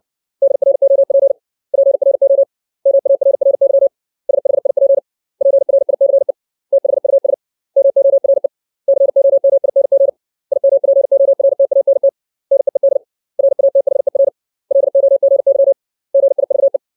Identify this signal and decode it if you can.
Morse code, heard as a keyed tone, about 60 wpm, 550 Hz. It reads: DROP PRO MARRY SHIP PRICE THUS MORE COMING IMPORTANT NEED RATHER LOGY GIVE
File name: 632 words / 632 Words 60WPM